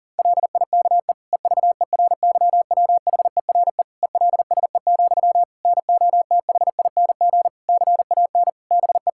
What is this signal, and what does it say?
Morse code, heard as a keyed tone, about 40 wpm, 700 Hz. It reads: LIKE EVERYWHERE ELSE, NOTHING CAN BE
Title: Sentences from Top 1600 Words 40wpm